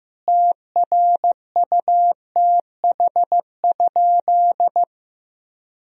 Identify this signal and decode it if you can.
Morse code, heard as a keyed tone, about 15 wpm, 700 Hz. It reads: TRUTH?
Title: Sentences from Top 1000 Words - Review Part 3 - 15wpm